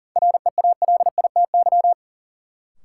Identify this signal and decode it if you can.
Morse code, heard as a keyed tone, about 40 wpm, 700 Hz. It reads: REALITY
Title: Sentences from Top 1200 Words - New Word Drill - 40wpm